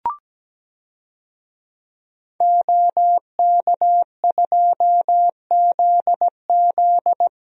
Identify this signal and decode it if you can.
Morse code, heard as a keyed tone, about 17 wpm, 700 Hz. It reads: OK2ZZ
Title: DXCC-100 Call Signs - 5 Characters - 17wpm